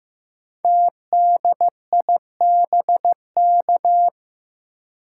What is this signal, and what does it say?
Morse code, heard as a keyed tone, about 15 wpm, 700 Hz. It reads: TDIBK
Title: Code Groups - 5 Letters - 15wpm